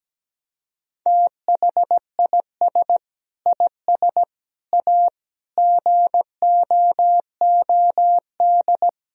Morse code, heard as a keyed tone, about 17 wpm, 700 Hz. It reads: THIS IS A GOOD
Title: Sentences from Top 1000 Words - Review Part 3 - 17wpm